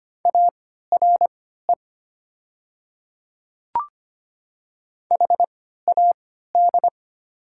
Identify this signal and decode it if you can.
Morse code, heard as a keyed tone, about 25 wpm, spaced as for 14 wpm, 700 Hz. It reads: ARE HAD
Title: 3x Character Spacing - Top 100 Words - 25wpm